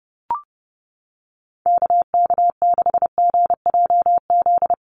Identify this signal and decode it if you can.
Morse code, heard as a keyed tone, about 30 wpm, 700 Hz. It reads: KK6GJZ
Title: USA Call Signs Repeated 3x 30wpm